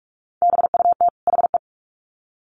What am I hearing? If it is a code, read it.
Morse code, heard as a keyed tone, about 44 wpm, 700 Hz. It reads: BUT HE
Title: Sets of 2 Words 45wpm